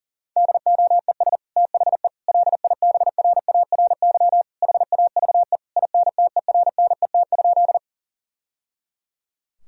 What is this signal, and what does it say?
Morse code, heard as a keyed tone, about 37 wpm, 700 Hz. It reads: DOES THE LIBRARY HAVE INTERNET?
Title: Sentences from Top 1000 Words - Review Part 4 - 40wpm